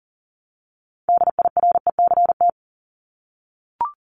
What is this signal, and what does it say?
Morse code, heard as a keyed tone, about 40 wpm, 700 Hz. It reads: DIRECT
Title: LY Suffix - Top 100 Words with Base Words 40wpm